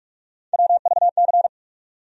Morse code, heard as a keyed tone, about 41 wpm, 700 Hz. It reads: WVC
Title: USA Call Sign Postfixes - 3 Letters 45wpm